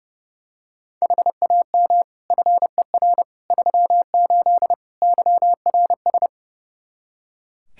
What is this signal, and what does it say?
Morse code, heard as a keyed tone, about 30 wpm, 700 Hz. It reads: HAM FER 38 YRS
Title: K1USN SST QSO Practice 30wpm